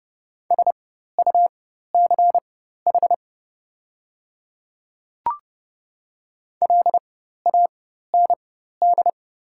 Morse code, heard as a keyed tone, about 31 wpm, spaced as for 14 wpm, 700 Hz. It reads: SUCH LAND
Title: 4x Character Spacing - Top 200 Words - 30wpm